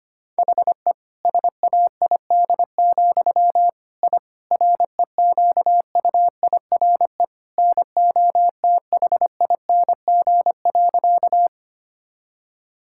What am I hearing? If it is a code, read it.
Morse code, heard as a keyed tone, about 25 wpm, 700 Hz. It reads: HE SAID, I REQUIRE NOTHING.
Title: Sentences from Top 1000 Words - Review Part 6 - 25wpm